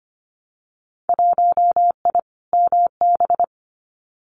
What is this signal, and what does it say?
Morse code, heard as a keyed tone, about 25 wpm, 700 Hz. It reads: 1I MB
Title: Winter Field Day Exchanges Encore 25wpm